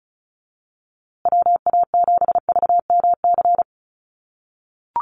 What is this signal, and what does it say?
Morse code, heard as a keyed tone, about 34 wpm, 700 Hz. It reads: WA7VMC
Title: USA Call Signs Repeated 3x 35wpm